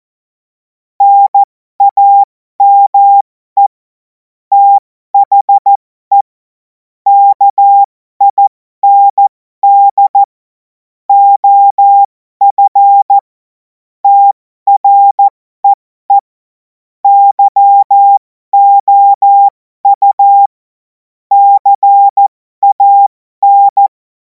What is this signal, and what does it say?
Morse code, heard as a keyed tone, about 14 wpm, spaced as for 12 wpm, 800 Hz. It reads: NAME THE KIND OF TREE YOU CAN
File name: Morsekurs / Witze